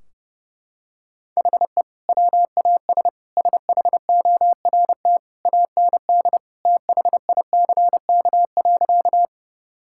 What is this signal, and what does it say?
Morse code, heard as a keyed tone, about 32 wpm, 700 Hz. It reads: HE WAS SHORT AND THICK.
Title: Sentences from Top 1600 Words 30wpm